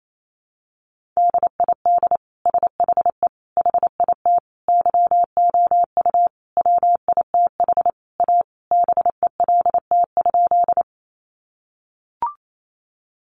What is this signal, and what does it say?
Morse code, heard as a keyed tone, about 28 wpm, 700 Hz. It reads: DID SHE HIT YOU WITH A BELT?
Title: Sentences from Top 1400 Words 28wpm